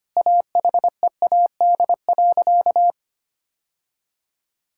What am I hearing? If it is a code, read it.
Morse code, heard as a keyed tone, about 25 wpm, 700 Hz. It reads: AHEAD.
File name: practice / 25wpm / Sentences with 2-Word Phrases - 25wpm